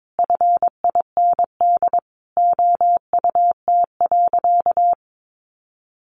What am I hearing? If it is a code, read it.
Morse code, heard as a keyed tone, about 22 wpm, 700 Hz. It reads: FIND OUT.
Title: Sentences from Top 500 Words 22wpm